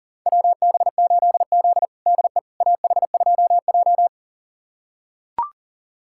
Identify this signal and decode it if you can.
Morse code, heard as a keyed tone, about 40 wpm, 700 Hz. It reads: WB8Z DE AH2J
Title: USA Callsign DE Callsign - 4 Characters - 40wpm